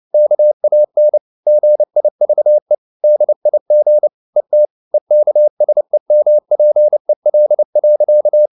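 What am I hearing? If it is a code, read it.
Morse code, heard as a keyed tone, about 29 wpm, 600 Hz. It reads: KAN GIVE DIG ET EKSEMPEL.
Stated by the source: Fast (QRQ) Morse at 29 wpm, from CW med Gnister.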